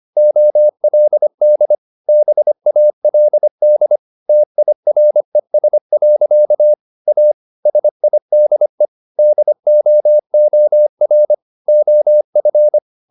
Morse code, heard as a keyed tone, about 25 wpm, 600 Hz. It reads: OLD BALD TIRES. A SIDE DOOR OF